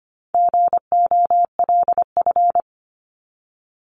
Morse code, heard as a keyed tone, about 25 wpm, 700 Hz. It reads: GOLF
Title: Top 1751-2000 Words 25wpm